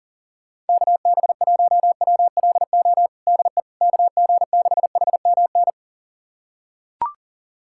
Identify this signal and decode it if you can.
Morse code, heard as a keyed tone, about 40 wpm, 700 Hz. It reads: KB1WLO DE KG6HMN
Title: USA Callsign DE Callsign 40wpm